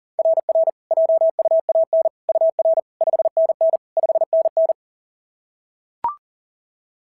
Morse code, heard as a keyed tone, about 40 wpm, 690 Hz. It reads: RR JUAN UR 5NN 5NN
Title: SOTA QSO Practice 40wpm